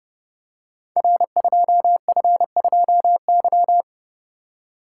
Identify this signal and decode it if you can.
Morse code, heard as a keyed tone, about 30 wpm, 700 Hz. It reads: R2F2Y
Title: Code Groups - 5 Letter:Numbers - 30wpm